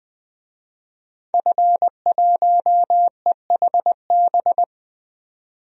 Morse code, heard as a keyed tone, about 20 wpm, 700 Hz. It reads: F1EHB
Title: DXCC-100 Call Signs - 5 Characters - 20wpm